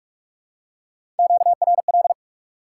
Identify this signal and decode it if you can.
Morse code, heard as a keyed tone, about 43 wpm, 700 Hz. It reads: QRL
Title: Top 180 QSO Elements 45wpm